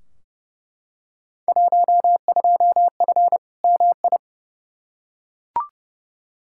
Morse code, heard as a keyed tone, about 31 wpm, 700 Hz. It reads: 12F MI
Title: Field Day Exchanges Encore - 30wpm